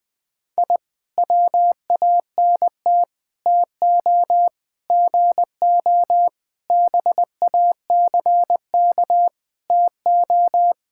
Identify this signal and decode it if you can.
Morse code, heard as a keyed tone, about 20 wpm, 700 Hz. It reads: I WANT TO GO BACK TO
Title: Sentences with 4-Word Phrases - 20wpm